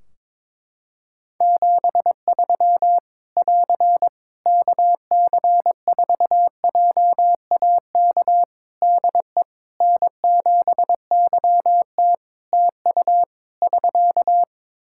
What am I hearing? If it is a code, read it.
Morse code, heard as a keyed tone, about 22 wpm, 700 Hz. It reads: 73 <AR> KC4JAK DE N7YT TU <SK>
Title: Practice QSO - Exchange 4 - Ending - 22wpm